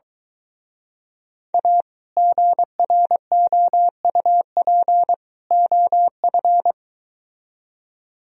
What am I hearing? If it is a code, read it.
Morse code, heard as a keyed tone, about 23 wpm, 700 Hz. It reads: A GROUP OF